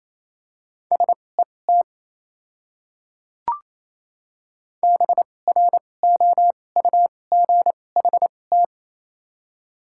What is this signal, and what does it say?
Morse code, heard as a keyed tone, about 28 wpm, spaced as for 20 wpm, 700 Hz. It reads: SET BROUGHT
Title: 2x Character Spacing - Top 500 Words - 28wpm